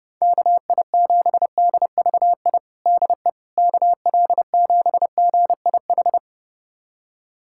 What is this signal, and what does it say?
Morse code, heard as a keyed tone, about 32 wpm, 700 Hz. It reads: KI7DVI DE KL7GIH
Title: USA Callsign DE Callsign - 6 Characters - 30wpm